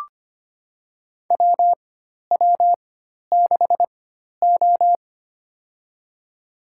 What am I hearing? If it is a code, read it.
Morse code, heard as a keyed tone, about 25 wpm, spaced as for 12 wpm, 700 Hz. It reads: WW6O
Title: USA Call Signs - 4 Characters - 4x Character Spacing - 25wpm